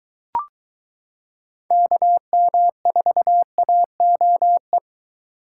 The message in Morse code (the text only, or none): KM4AOE